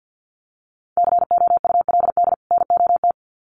Morse code, heard as a keyed tone, about 50 wpm, 700 Hz. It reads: COULD NOT
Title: Sets of 2 Words 50wpm